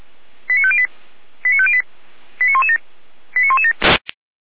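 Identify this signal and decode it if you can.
Five-tone selective call (ZVEI), heard as quick runs of five tones, tones of about 70 ms. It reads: 89508 89508 89117 89117
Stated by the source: ZVEI five-tone selective call 89115 received with an SDR.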